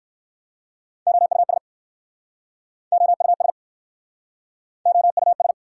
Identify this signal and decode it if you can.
Morse code, heard as a keyed tone, about 61 wpm, 700 Hz. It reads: OUS OUS OUS